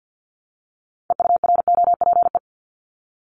Morse code, heard as a keyed tone, about 50 wpm, 700 Hz. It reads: EUROPE
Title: Multi-letter Phonemes Repeated 3x 50wpm